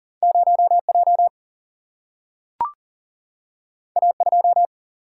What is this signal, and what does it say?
Morse code, heard as a keyed tone, about 40 wpm, 700 Hz. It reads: 0J A2
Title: Single-Letter Single-Number 40wpm